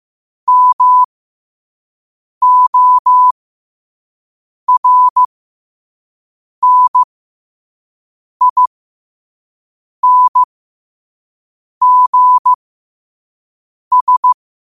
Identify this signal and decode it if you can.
Morse code, heard as a keyed tone, about 15 wpm, spaced as for 5 wpm, 1000 Hz. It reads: MORNINGS
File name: qotdfiles / 5wpm